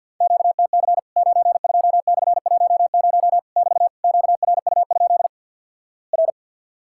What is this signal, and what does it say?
Morse code, heard as a keyed tone, about 50 wpm, 700 Hz. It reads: QTC 92/10 <BT> QRU? R
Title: Worked All Europe DX Contest - QTC Exchange - 50wpm